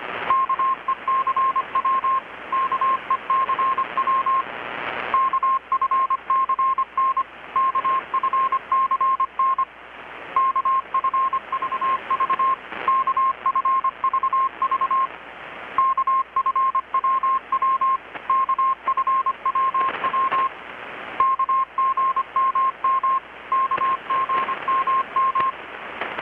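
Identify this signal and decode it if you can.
Morse code, heard as a keyed tone, about 25 wpm, 1060 Hz. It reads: KECW KECW KFCN KFCN KFVV KFVV KFWW KFWW KGMM KGMM
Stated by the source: Morse from KPH's Night of Nights XX maritime radiotelegraphy event, received in Chiapas, Mexico about 3,800 km away on a Kenwood R-600 with a grounded 10 m long-wire antenna.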